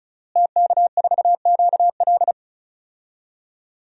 Morse code, heard as a keyed tone, about 36 wpm, 700 Hz. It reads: TK4QL
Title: DXCC-100 Call Signs - 5 Characters - Repeated 3x - 35wpm